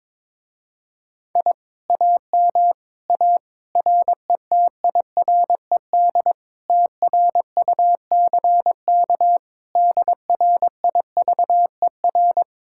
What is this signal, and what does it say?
Morse code, heard as a keyed tone, about 22 wpm, 700 Hz. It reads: I AM A RETIRED TRUCK DRIVER
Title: Speed-Racing Common QSO Phrases - 22wpm